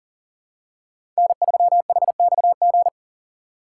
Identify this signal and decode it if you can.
Morse code, heard as a keyed tone, about 40 wpm, 700 Hz. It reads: N3HXG
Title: USA Call Signs - 5 Characters - 3x Character Spacing - 40wpm